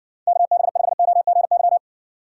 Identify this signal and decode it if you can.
Morse code, heard as a keyed tone, about 60 wpm, 700 Hz. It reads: KB5OGY